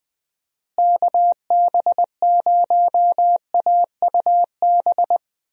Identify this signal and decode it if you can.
Morse code, heard as a keyed tone, about 21 wpm, 700 Hz. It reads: KB0AUB
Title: USA Call Signs - 6 Characters - Repeated 3x - 20wpm